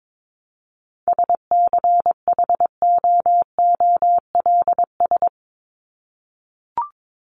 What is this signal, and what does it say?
Morse code, heard as a keyed tone, about 22 wpm, 700 Hz. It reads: SCHOOLS
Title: ES:S Suffix - Top 100 Words with Base Words 22wpm